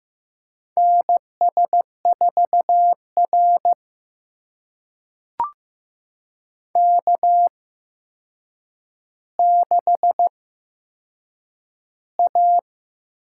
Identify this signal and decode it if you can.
Morse code, heard as a keyed tone, about 15 wpm, spaced as for 7 wpm, 700 Hz. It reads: NS4R K6A
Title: USA Call Signs - 4 Characters - 8x Character Spacing - 15wpm